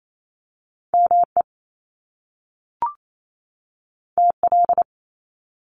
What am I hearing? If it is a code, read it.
Morse code, heard as a keyed tone, about 28 wpm, 700 Hz. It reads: ME TL
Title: Top Two Letter Combinations in Words 28wpm